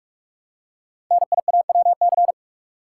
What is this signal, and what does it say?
Morse code, heard as a keyed tone, about 44 wpm, 700 Hz. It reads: NIAWC
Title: Code Groups - 5 Letters - 45wpm